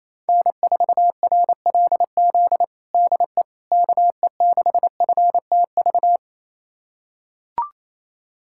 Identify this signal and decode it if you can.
Morse code, heard as a keyed tone, about 28 wpm, 700 Hz. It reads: N4RLZ DE KE6FTV
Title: USA Callsign DE Callsign 28wpm